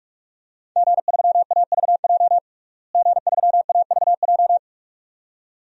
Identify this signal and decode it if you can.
Morse code, heard as a keyed tone, about 46 wpm, 700 Hz. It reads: G3AVJ G3AVJ
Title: DXCC-100 Call Signs - 5 Characters - Repeated 2x - 45wpm